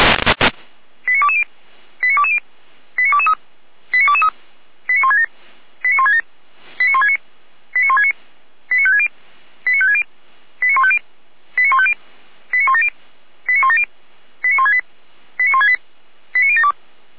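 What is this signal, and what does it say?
Five-tone selective call (ZVEI), heard as quick runs of five tones, tones of about 70 ms. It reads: 89220 89220 89223 89223 89167 89167 89169 89169 89560 89560 89150 89150 89170 89170 89168 89168 89072
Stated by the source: Five-tone ZVEI selective call received by SDR, audio from feuerwehr-orlamuende.org (Feuerwehr Orlamünde).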